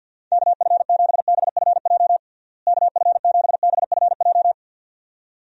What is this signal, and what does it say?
Morse code, heard as a keyed tone, about 50 wpm, 700 Hz. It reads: KF7BFJ KF7BFJ
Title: USA Call Signs - 6 Characters - Repeated 2x - 50wpm